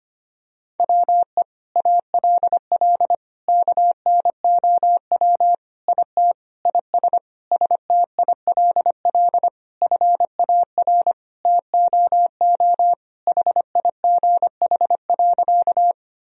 Morse code, heard as a keyed tone, about 25 wpm, 700 Hz. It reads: WE ALL KNOW IT IS STILL FAR TOO HIGH.
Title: Sentences from Top 1000 Words - Review Part 1 - 25wpm